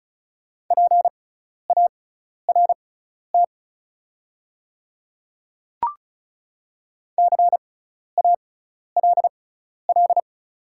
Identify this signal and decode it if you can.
Morse code, heard as a keyed tone, about 36 wpm, spaced as for 12 wpm, 690 Hz. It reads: PART CALL
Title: 6x Character Spacing - Top 200 Words - 35wpm